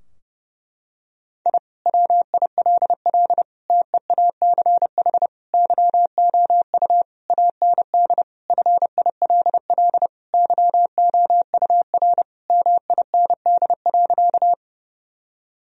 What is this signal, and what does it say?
Morse code, heard as a keyed tone, about 30 wpm, 700 Hz. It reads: I WILL TEACH YOU AND FILL YOUR MIND.
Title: Sentences from Top 1000 Words - Review Part 3 - 30wpm